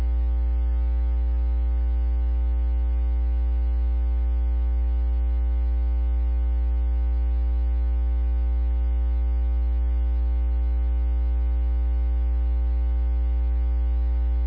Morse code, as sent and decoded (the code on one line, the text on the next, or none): none